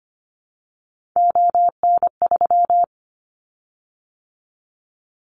Morse code, heard as a keyed tone, about 25 wpm, 700 Hz. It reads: ON3
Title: Callsign Prefixes for Top 50 DXCC - 25wpm